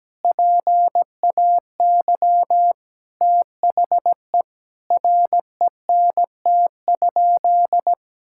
Morse code, heard as a keyed tone, about 17 wpm, 700 Hz. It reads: PAY THE RENT?
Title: Sentences from Top 1000 Words - Review Part 5 - 17wpm